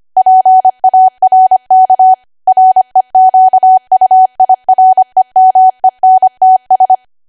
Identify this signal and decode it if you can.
Morse code, heard as a keyed tone, about 25 wpm, 750 Hz. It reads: PARK REQUIREMENTS
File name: Archive / 25 WPM / 25WPM